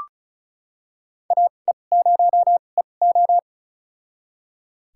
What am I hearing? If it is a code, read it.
Morse code, heard as a keyed tone, about 36 wpm, spaced as for 25 wpm, 700 Hz. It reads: AE0EO
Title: USA Call Signs - 5 Characters - 2x Character Spacing - 35wpm